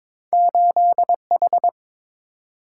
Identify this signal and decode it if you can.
Morse code, heard as a keyed tone, about 22 wpm, 700 Hz. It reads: 8H
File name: practice / 22wpm / Two Letter-Number Combinations - 22wpm